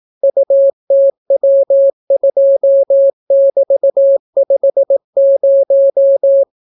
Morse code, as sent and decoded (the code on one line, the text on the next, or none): ..- - .-- ..--- -...- ..... -----
UTW2<BT>50